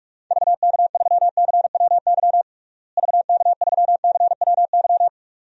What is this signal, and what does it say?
Morse code, heard as a keyed tone, about 46 wpm, 700 Hz. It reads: VK3CWY VK3CWY
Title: DXCC-100 Call Signs - 6 Characters - Repeated 2x - 45wpm